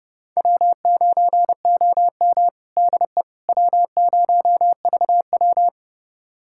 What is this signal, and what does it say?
Morse code, heard as a keyed tone, about 32 wpm, 700 Hz. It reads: W9OM DE W0VW
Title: USA Callsign DE Callsign - 4 Characters - 30wpm